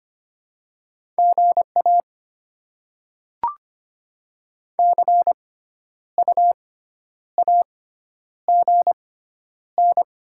Morse code, heard as a keyed tone, about 25 wpm, spaced as for 10 wpm, 700 Hz. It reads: GA CU AGN
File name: Top 20 QSO Elements - 6x Character Spacing - 25wpm